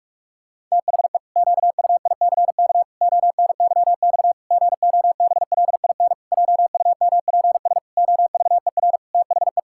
Morse code, heard as a keyed tone, about 41 wpm, 700 Hz. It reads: THE QUICK ONYX GOBLIN JUMPS OVER THE
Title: Pangram Challenge v2 - 45wpm